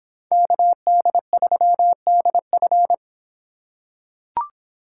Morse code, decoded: KD3DF

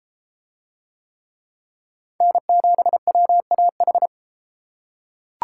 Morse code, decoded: N7WAH

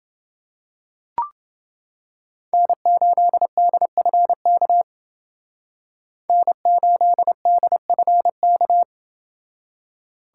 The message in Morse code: N8DFK N8DFK

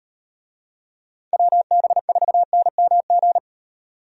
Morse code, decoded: WB4NMG

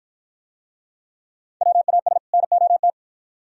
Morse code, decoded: WAS NOT

